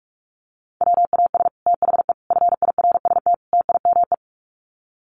Morse code, heard as a keyed tone, about 44 wpm, 700 Hz. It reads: WAS THE FIRST TIME
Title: Sets of 4 Words 45wpm